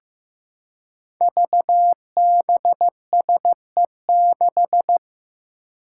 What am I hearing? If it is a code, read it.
Morse code, heard as a keyed tone, about 15 wpm, 700 Hz. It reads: VBSE6
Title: Five Challenging Letter-Number Combinations 15wpm